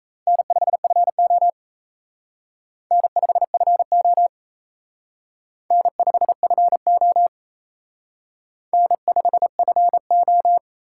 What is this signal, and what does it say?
Morse code, heard as a keyed tone, about 35 wpm, 700 Hz. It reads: N5FO N5FO N5FO N5FO
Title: Speed Racing - USA Call Signs 28wpm